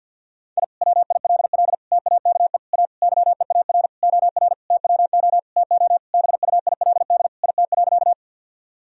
Morse code, heard as a keyed tone, about 51 wpm, 700 Hz. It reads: I WILL TAKE A YEAR OR TWO TO BUILD IT.
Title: Sentences from Top 1000 Words - Review Part 1 - 50wpm